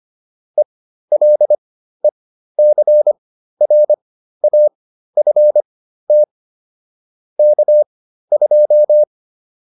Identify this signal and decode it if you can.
Morse code, heard as a keyed tone, about 25 wpm, spaced as for 13 wpm, 600 Hz. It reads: ELECRAFT K2